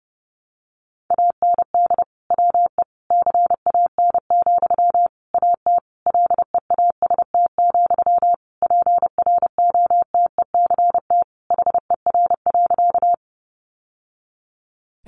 Morse code, heard as a keyed tone, about 30 wpm, 700 Hz. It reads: AND WE CAN, AT LEAST, PROTECT HER.
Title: Sentences from Top 900 Words 30wpm